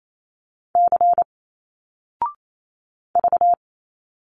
Morse code, heard as a keyed tone, about 28 wpm, 700 Hz. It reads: C V
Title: Single Letters 28wpm